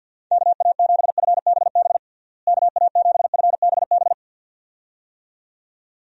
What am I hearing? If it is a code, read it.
Morse code, heard as a keyed tone, about 51 wpm, 700 Hz. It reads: KA7FBB KA7FBB
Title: USA Call Signs - 6 Characters - Repeated 2x - 50wpm